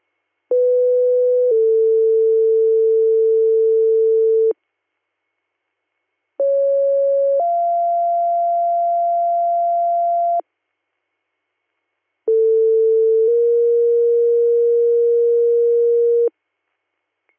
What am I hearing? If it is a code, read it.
Two-tone page, heard as a one-second tone, then a three-second tone, about 500 then 450 Hz, about 550 then 690 Hz, about 450 then 470 Hz. It is heard from 0.5 to 16.3 s.